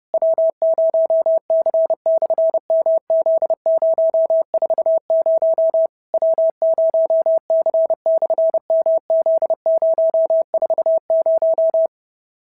Morse code, heard as a keyed tone, about 30 wpm, 650 Hz. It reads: W0C/MZ040 W0C/MZ040
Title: SOTA QSO Practice 30wpm